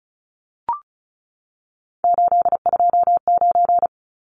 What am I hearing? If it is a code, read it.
Morse code, heard as a keyed tone, about 34 wpm, 700 Hz. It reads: 829
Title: Numbers - 3-Digit 35wpm